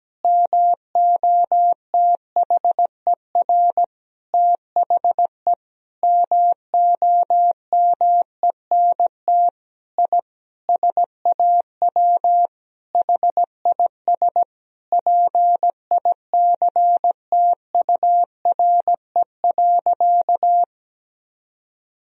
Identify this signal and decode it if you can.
Morse code, heard as a keyed tone, about 17 wpm, 700 Hz. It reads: MOTHER THE MOMENT I SAW HIS PICTURE.
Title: Sentences from Top 1000 Words - Review Part 1 - 17wpm